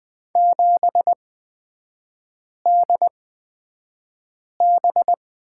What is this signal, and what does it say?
Morse code, heard as a keyed tone, about 20 wpm, spaced as for 5 wpm, 700 Hz. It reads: 7DB
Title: 250 Lesson 40e - BK - Three Characters 20wpm